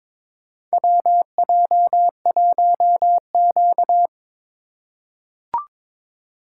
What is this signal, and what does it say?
Morse code, heard as a keyed tone, about 22 wpm, 700 Hz. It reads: WJ1Q
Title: USA Call Signs - 4 Characters - 3x Character Spacing - 22wpm